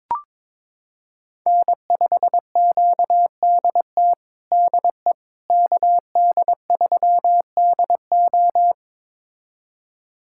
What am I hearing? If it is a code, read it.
Morse code, heard as a keyed tone, about 21 wpm, 700 Hz. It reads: N5QDT DE KD3DO
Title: USA Callsign DE Callsign - 5 Characters - 22wpm